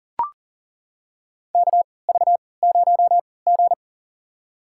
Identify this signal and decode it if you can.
Morse code, heard as a keyed tone, about 40 wpm, spaced as for 23 wpm, 700 Hz. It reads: KV0G
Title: USA Call Signs - 4 Characters - 3x Character Spacing - 40wpm